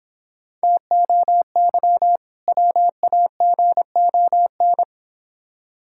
Morse code, heard as a keyed tone, about 23 wpm, 690 Hz. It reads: TOY WAGON
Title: Difficult Words - Speed-Racing with Hints - 20wpm